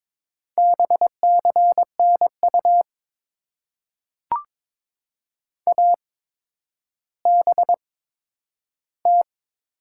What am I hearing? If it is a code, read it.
Morse code, heard as a keyed tone, about 22 wpm, spaced as for 11 wpm, 690 Hz. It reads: BCNU ABT
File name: Top 90 QSO Elements - 8x Character Spacing - 22wpm